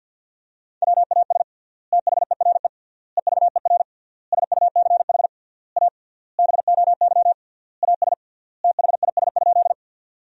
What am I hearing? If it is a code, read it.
Morse code, heard as a keyed tone, about 50 wpm, 700 Hz. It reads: WAS THERE EVER SUCH A BOY AS THIS?
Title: 3x Word-Spacing - Sentences from Top 400 Words - 50wpm